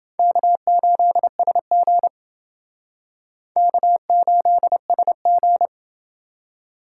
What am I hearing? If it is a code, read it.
Morse code, heard as a keyed tone, about 30 wpm, 700 Hz. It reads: K8SG K8SG